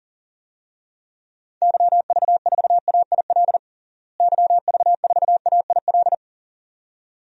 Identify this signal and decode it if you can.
Morse code, heard as a keyed tone, about 40 wpm, 700 Hz. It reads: YV4AIL YV4AIL
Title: DXCC-100 Call Signs - 6 Characters - Repeated 2x - 40wpm